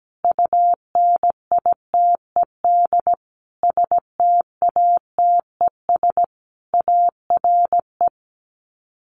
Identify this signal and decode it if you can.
Morse code, heard as a keyed tone, about 17 wpm, 690 Hz. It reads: UNITED STATES ARE
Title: Sets of 5 Words Encore 17wpm